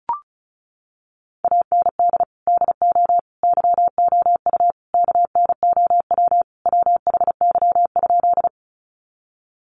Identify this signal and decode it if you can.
Morse code, heard as a keyed tone, about 34 wpm, 700 Hz. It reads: AND DO YOU KNOW WHY?
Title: Sentences from Top 200 Words 35wpm